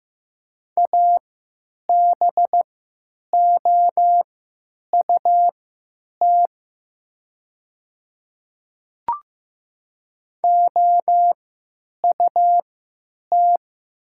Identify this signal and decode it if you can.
Morse code, heard as a keyed tone, about 15 wpm, spaced as for 9 wpm, 700 Hz. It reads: ABOUT OUT
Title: 3x Character Spacing - Top 100 Words - 15wpm